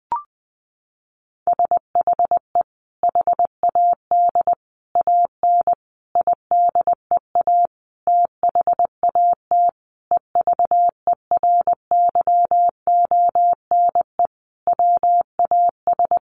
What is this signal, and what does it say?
Morse code, heard as a keyed tone, about 20 wpm, 700 Hz. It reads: SHE HAD AN IDEA THAT EVERYONE WAS
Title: Sentences from Top 1400 Words 20wpm